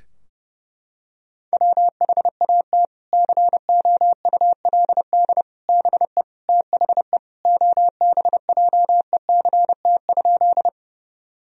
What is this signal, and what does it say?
Morse code, heard as a keyed tone, about 30 wpm, 700 Hz. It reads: WHAT COULD BE THE OBJECT?
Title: Sentences from Top 1000 Words - Review Part 3 - 30wpm